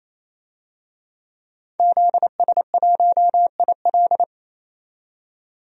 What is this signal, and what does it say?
Morse code, heard as a keyed tone, about 27 wpm, 690 Hz. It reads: ZS1IL
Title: DXCC-100 Call Signs - 5 Characters - 28wpm